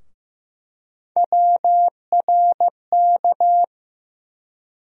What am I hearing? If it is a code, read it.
Morse code, heard as a keyed tone, about 15 wpm, 700 Hz. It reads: WRK
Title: USA Call Sign Postfixes - 3 Letters 15wpm